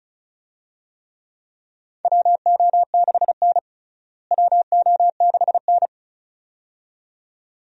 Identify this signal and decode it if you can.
Morse code, heard as a keyed tone, about 36 wpm, 700 Hz. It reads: WO6N WO6N
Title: USA Call Signs - 4 Characters - Repeated 2x - 35wpm